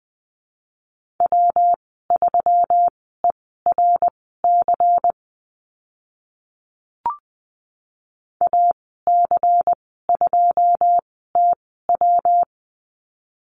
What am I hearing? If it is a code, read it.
Morse code, heard as a keyed tone, about 20 wpm, spaced as for 14 wpm, 700 Hz. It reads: W3ERC AC2TW
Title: 122 Lesson 20g - Number 3 - Call Signs 20wpm